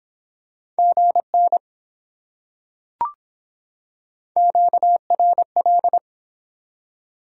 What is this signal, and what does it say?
Morse code, heard as a keyed tone, about 24 wpm, 700 Hz. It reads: GN QRL